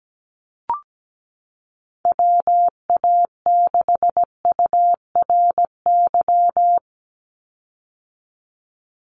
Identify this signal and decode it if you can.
Morse code, heard as a keyed tone, about 17 wpm, 700 Hz. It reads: WA6URY
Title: Call Signs - 17wpm